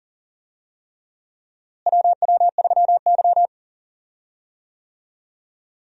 Morse code, heard as a keyed tone, about 40 wpm, 700 Hz. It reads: WW3Y
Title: USA Call Signs - 4 Characters - Repeated 3x - 40wpm